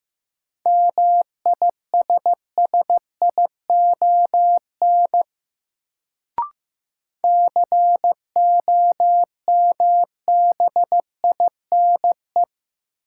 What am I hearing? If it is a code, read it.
Morse code, heard as a keyed tone, about 15 wpm, 700 Hz. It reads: MISSION COMBINE
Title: Sentences from Top 1600 Words - New Word Drill - 15wpm